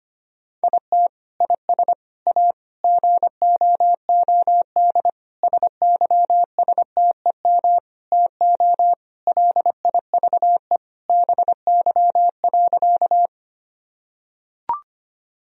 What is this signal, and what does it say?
Morse code, heard as a keyed tone, about 25 wpm, 700 Hz. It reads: IT IS A GOOD SYSTEM TO LIVE BY.
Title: Sentences from Top 1000 Words - Review Part 3 - 25wpm